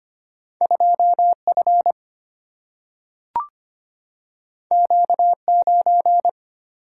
Morse code, heard as a keyed tone, about 25 wpm, 700 Hz. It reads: 2 F Q 9
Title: Single-Letter Single-Number 25wpm